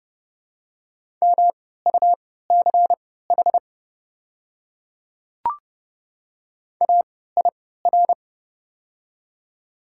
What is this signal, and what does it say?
Morse code, heard as a keyed tone, about 32 wpm, spaced as for 17 wpm, 690 Hz. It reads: MUCH AIR
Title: 3x Character Spacing - Top 200 Words - 30wpm